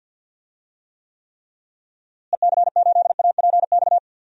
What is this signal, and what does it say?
Morse code, heard as a keyed tone, about 52 wpm, 700 Hz. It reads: EC8APX
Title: DXCC-100 Call Signs - 6 Characters - Repeated 3x - 50wpm